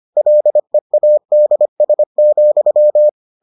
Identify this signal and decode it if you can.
Morse code, heard as a keyed tone, about 25 wpm, 600 Hz. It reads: LEADS,